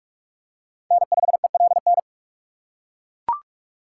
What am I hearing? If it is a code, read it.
Morse code, heard as a keyed tone, about 41 wpm, 700 Hz. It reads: N5ELN